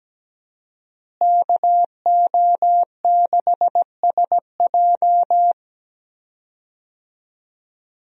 Morse code, heard as a keyed tone, about 17 wpm, 700 Hz. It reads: KO6SJ
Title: USA Call Signs - 5 Characters - Repeated 2x - 17wpm